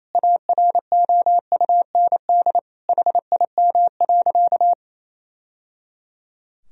Morse code, heard as a keyed tone, about 28 wpm, 700 Hz. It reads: AROUND HIM.
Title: Sentences from Top 1000 Words - Review Part 3 - 28wpm